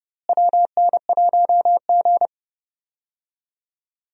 Morse code, heard as a keyed tone, about 30 wpm, 700 Hz. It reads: WN1G
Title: USA Call Signs - 4 Characters - Repeated 3x - 30wpm